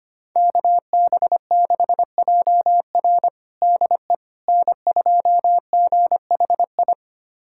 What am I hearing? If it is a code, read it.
Morse code, heard as a keyed tone, about 25 wpm, 700 Hz. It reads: KB6JR DE N2GHI
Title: USA Callsign DE Callsign 25wpm